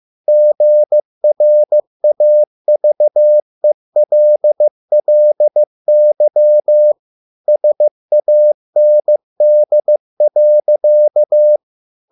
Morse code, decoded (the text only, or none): GRAVELLY SAND.